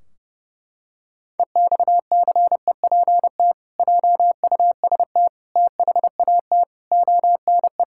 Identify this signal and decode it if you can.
Morse code, heard as a keyed tone, about 30 wpm, 700 Hz. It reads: EXCEPT JUST THAT ONE
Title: Sentences from Top 1600 Words 30wpm